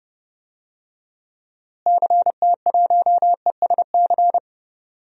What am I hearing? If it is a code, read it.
Morse code, heard as a keyed tone, about 30 wpm, 700 Hz. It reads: CT1ESC
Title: DXCC-100 Call Signs - 6 Characters - Repeated 2x - 30wpm